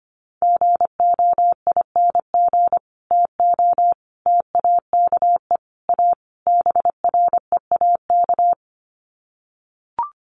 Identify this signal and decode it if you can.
Morse code, heard as a keyed tone, about 25 wpm, 700 Hz. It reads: GOING TO TAKE A BREAK
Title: Sets of 5 Words 25wpm